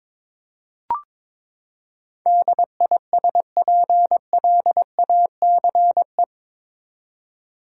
Morse code, heard as a keyed tone, about 22 wpm, 700 Hz. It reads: DISPLACE
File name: practice / 22wpm / Eight Letter Words - 22wpm